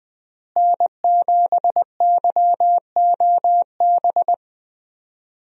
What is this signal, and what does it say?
Morse code, heard as a keyed tone, about 21 wpm, 700 Hz. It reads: N7YOB
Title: USA Call Signs - 5 Characters - Repeated 2x - 20wpm